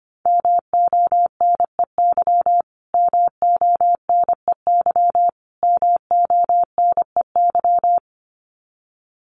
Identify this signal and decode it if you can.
Morse code, heard as a keyed tone, about 25 wpm, 700 Hz. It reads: MONEY MONEY MONEY
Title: Top 500 Words Repeated 25wpm